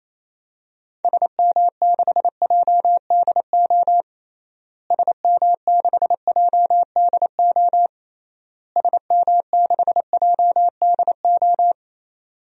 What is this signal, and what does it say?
Morse code, heard as a keyed tone, about 28 wpm, 700 Hz. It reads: SM6JDO SM6JDO SM6JDO
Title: DXCC-100 Call Signs - 6 Characters - Repeated 3x - 28wpm